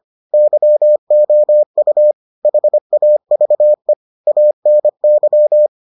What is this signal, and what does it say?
Morse code, heard as a keyed tone, about 25 wpm, 600 Hz. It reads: YOU HAVE ANY